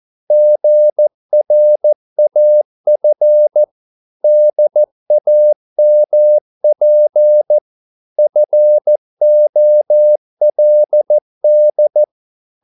Morse code, heard as a keyed tone, about 14 wpm, 600 Hz. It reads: GRAF DAMP FOLD
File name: Korte ord DK 14wpm